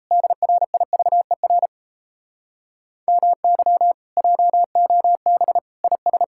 Morse code, heard as a keyed tone, about 38 wpm, 700 Hz. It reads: DRIVER MY JOB IS